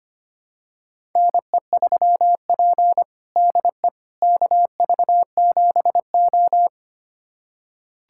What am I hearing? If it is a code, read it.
Morse code, heard as a keyed tone, about 25 wpm, 700 Hz. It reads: NE3P DE KV7O
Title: USA Callsign DE Callsign - 4 Characters - 25wpm